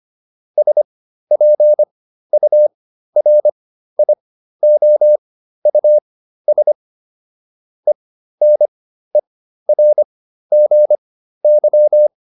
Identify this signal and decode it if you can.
Morse code, heard as a keyed tone, about 25 wpm, spaced as for 13 wpm, 600 Hz. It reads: SPURIOUS ENERGY